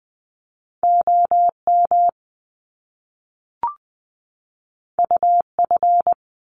Morse code, heard as a keyed tone, about 20 wpm, 700 Hz. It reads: OM UF
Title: Top Two Letter Combinations in Words 20wpm